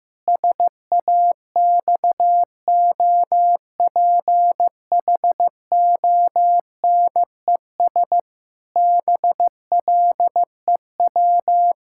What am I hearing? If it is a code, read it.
Morse code, heard as a keyed tone, about 16 wpm, 700 Hz. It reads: SAXOPHONES BLEW
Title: Pangram Challenge v2 - 15wpm